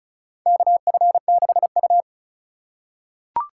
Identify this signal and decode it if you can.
Morse code, heard as a keyed tone, about 34 wpm, 700 Hz. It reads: KF6U
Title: USA Call Signs - 4 Characters - 2x Character Spacing - 35wpm